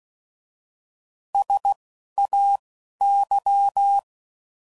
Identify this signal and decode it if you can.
Morse code, heard as a keyed tone, about 16 wpm, spaced as for 11 wpm, 800 Hz. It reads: SAY
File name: Prefixes / By First Letter / S vs H